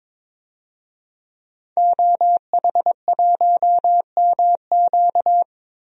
Morse code, heard as a keyed tone, about 22 wpm, 700 Hz. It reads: OH1MQ
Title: DXCC-100 Call Signs - 5 Characters - 22wpm